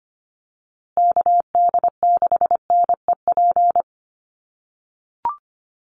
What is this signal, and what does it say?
Morse code, heard as a keyed tone, about 25 wpm, 700 Hz. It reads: KD6NEP
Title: USA Call Signs Repeated 3x 25wpm